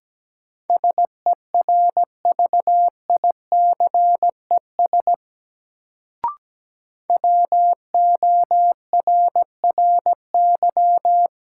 Morse code, heard as a keyed tone, about 17 wpm, 700 Hz. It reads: SERVICES WORRY
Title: Sets of 5 Words - New Word Drill - 17wpm